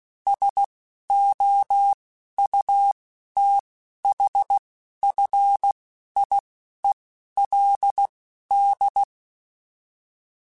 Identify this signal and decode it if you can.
Morse code, heard as a keyed tone, about 16 wpm, spaced as for 11 wpm, 800 Hz. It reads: SOUTHFIELD